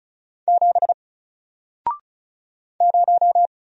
Morse code, heard as a keyed tone, about 33 wpm, 700 Hz. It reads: 7 0
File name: Single-Letter-Number-35wpm